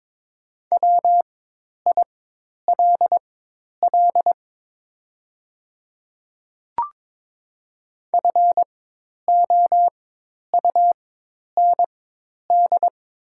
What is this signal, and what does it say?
Morse code, heard as a keyed tone, about 22 wpm, spaced as for 10 wpm, 700 Hz. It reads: WILL FOUND
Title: 4x Character Spacing - Top 500 Words - 22wpm